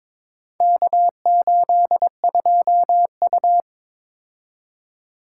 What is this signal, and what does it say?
Morse code, heard as a keyed tone, about 22 wpm, 700 Hz. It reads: K82U
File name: practice / 22wpm / Four Letter-Number Combinations - 22wpm